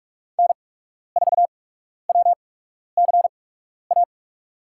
Morse code, heard as a keyed tone, about 44 wpm, spaced as for 12 wpm, 700 Hz. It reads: N4WCA
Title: USA Call Signs - 8x Character Spacing - 45wpm